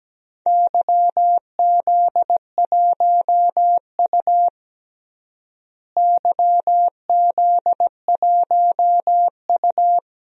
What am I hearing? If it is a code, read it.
Morse code, heard as a keyed tone, about 17 wpm, 700 Hz. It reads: YZ1U YZ1U
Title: DXCC-100 Call Signs - 4 Characters - Repeated 2x - 17wpm